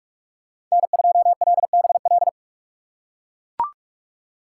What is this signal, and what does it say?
Morse code, heard as a keyed tone, about 41 wpm, 700 Hz. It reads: N2LBL
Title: USA Call Signs - 6x Character Spacing - 45wpm